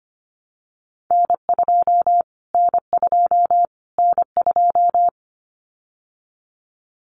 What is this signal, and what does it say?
Morse code, heard as a keyed tone, about 25 wpm, 700 Hz. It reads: N2 N2 N2
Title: Callsign Prefixes for Top 25 DXCC - Repeated 3x - 25wpm